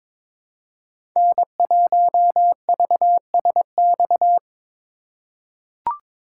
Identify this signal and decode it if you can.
Morse code, heard as a keyed tone, about 22 wpm, 700 Hz. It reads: N1VSX
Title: USA Call Signs - 5 Characters - 2x Character Spacing - 22wpm